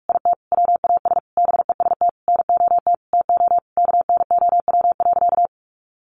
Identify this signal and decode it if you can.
Morse code, heard as a keyed tone, about 44 wpm, 700 Hz. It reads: IT WAS BEST NOT TO KNOW.
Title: Sentences from Top 600 Words 45wpm